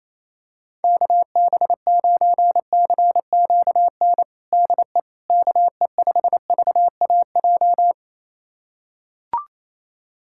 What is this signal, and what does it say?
Morse code, heard as a keyed tone, about 28 wpm, 700 Hz. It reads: KB9CQN DE KE5VAJ
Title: USA Callsign DE Callsign - 6 Characters - 28wpm